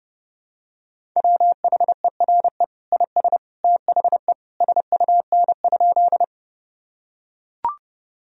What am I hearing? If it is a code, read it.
Morse code, heard as a keyed tone, about 30 wpm, 700 Hz. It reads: WHERE IS THE SUN?
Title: Sentences from Top 1000 Words - Review Part 1 - 30wpm